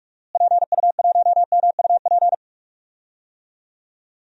Morse code, heard as a keyed tone, about 44 wpm, 700 Hz. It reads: PU1MUP
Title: DXCC-100 Call Signs - 6 Characters - Repeated 2x - 45wpm